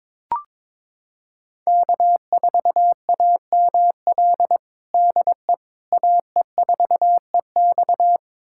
Morse code, heard as a keyed tone, about 22 wpm, 700 Hz. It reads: K4AML DE AE4EX
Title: USA Callsign DE Callsign - 5 Characters - 22wpm